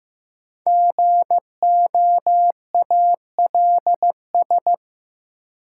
Morse code, heard as a keyed tone, about 15 wpm, 700 Hz. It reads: GOALS
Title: Sentences from Top 1000 Words - Review Part 6 - New Word Drill - 15wpm